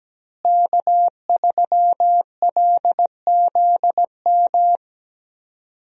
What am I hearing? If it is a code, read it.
Morse code, heard as a keyed tone, about 17 wpm, 700 Hz. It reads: K3LZM
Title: USA Call Signs - 5 Characters - Repeated 3x - 17wpm